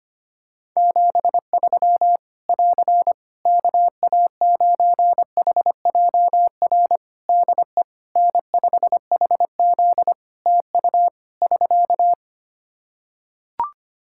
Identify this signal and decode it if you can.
Morse code, heard as a keyed tone, about 25 wpm, 700 Hz. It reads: 73 <AR> KA9HJR DE N5HZ TU <SK>
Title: Practice QSO - Exchange 4 - Ending - 25wpm